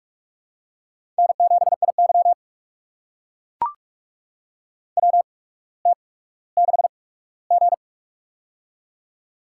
Morse code, heard as a keyed tone, about 44 wpm, spaced as for 19 wpm, 700 Hz. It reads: N7IY WT6G
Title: USA Call Signs - 8x Character Spacing - 45wpm